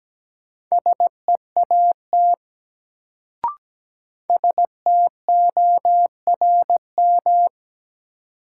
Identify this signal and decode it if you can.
Morse code, heard as a keyed tone, about 17 wpm, 700 Hz. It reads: SEAT STORM
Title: Sentences from Top 1500 Words - New Word Drill Alt - 17wpm